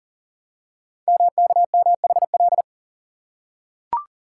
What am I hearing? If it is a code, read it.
Morse code, heard as a keyed tone, about 39 wpm, 690 Hz. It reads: MKMHL
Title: Code Groups - 5 Letters - 40wpm